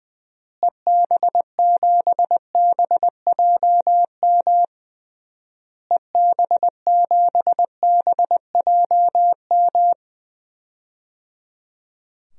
Morse code, heard as a keyed tone, about 20 wpm, 700 Hz. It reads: EB7BJM EB7BJM
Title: DXCC-100 Call Signs - 6 Characters - Repeated 3x - 20wpm